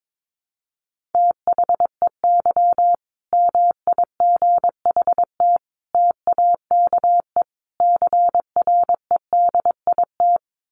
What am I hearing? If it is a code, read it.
Morse code, heard as a keyed tone, about 22 wpm, 700 Hz. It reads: THEY MIGHT TAKE CREDIT
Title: Sentences from Top 1000 Words 22wpm